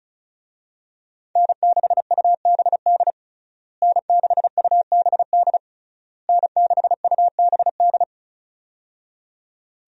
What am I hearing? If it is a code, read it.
Morse code, heard as a keyed tone, about 36 wpm, 700 Hz. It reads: N6UBD N6UBD N6UBD
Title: USA Call Signs - 5 Characters - Repeated 3x - 35wpm